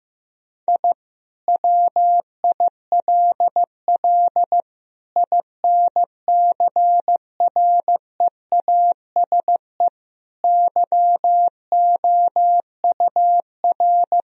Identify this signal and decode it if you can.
Morse code, heard as a keyed tone, about 15 wpm, 700 Hz. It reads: I WILL INCREASE YOUR
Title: Sentences from Top 1000 Words - Review Part 4 - 15wpm